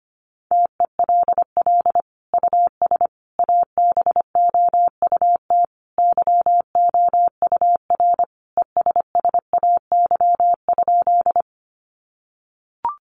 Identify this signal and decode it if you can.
Morse code, heard as a keyed tone, about 25 wpm, 700 Hz. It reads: TELL US ABOUT YOUR ESSAY?
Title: Sentences from Top 1500 Words 25wpm